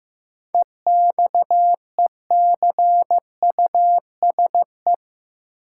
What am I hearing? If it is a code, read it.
Morse code, heard as a keyed tone, about 15 wpm, 700 Hz. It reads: EXECUSE
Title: Sentences from Top 700 Words - New Word Drill - 15wpm